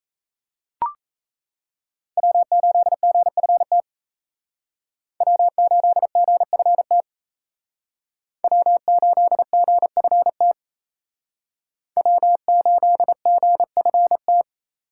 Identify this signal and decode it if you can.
Morse code, heard as a keyed tone, about 34 wpm, 700 Hz. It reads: W8GFT W8GFT W8GFT W8GFT
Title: Speed Racing - USA 1x3 Call Signs 28wpm